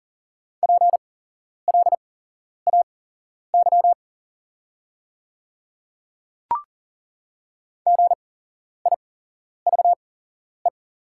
Morse code, heard as a keyed tone, about 40 wpm, spaced as for 11 wpm, 700 Hz. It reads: PLAY GIVE
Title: 8x Character Spacing - Top 200 Words - 40wpm